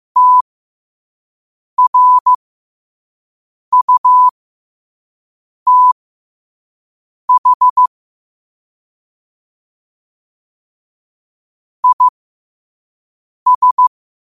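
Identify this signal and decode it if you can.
Morse code, heard as a keyed tone, about 15 wpm, spaced as for 5 wpm, 1000 Hz. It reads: TRUTH IS
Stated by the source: 5 WPM Morse quote of the day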